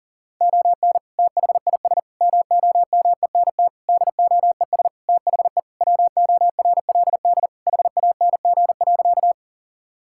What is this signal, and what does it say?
Morse code, heard as a keyed tone, about 40 wpm, 700 Hz. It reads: ON THIS MOMENT DOES THE WORLD HANG.
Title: Sentences from Top 1600 Words 40wpm